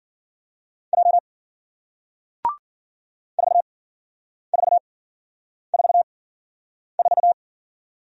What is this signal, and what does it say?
Morse code, heard as a keyed tone, about 52 wpm, spaced as for 9 wpm, 700 Hz. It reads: J 4444